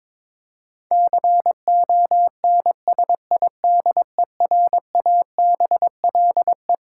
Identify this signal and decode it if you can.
Morse code, heard as a keyed tone, about 22 wpm, 700 Hz. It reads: CONSIDERABLE
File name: practice / 22wpm / Twelve Letter Words - 22wpm